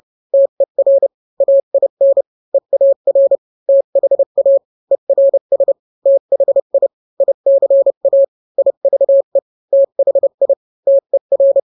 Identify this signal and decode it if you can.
Morse code, heard as a keyed tone, about 30 wpm, 550 Hz. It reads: TER AIN EAR THA ERS THI ICA IVE THI TER